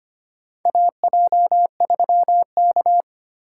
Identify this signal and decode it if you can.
Morse code, heard as a keyed tone, about 25 wpm, 700 Hz. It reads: AJ3K
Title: USA Call Signs - 6x Character Spacing - 25wpm